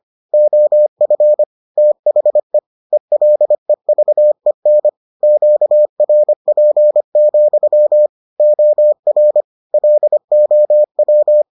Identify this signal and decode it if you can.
Morse code, heard as a keyed tone, about 25 wpm, 600 Hz. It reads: OF THE ELEVEN QRP, OR LOW